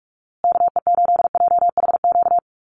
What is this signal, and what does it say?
Morse code, heard as a keyed tone, about 44 wpm, 700 Hz. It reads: KE8JHQ
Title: USA Call Signs Repeated 2x 45wpm